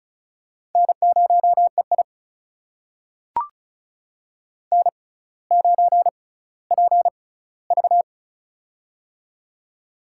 Morse code, heard as a keyed tone, about 36 wpm, spaced as for 18 wpm, 700 Hz. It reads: N0EI N9PV
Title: USA Call Signs - 4 Characters - 6x Character Spacing - 35wpm